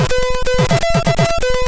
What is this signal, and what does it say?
There is no code.